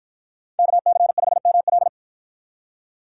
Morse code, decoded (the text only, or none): KC5ML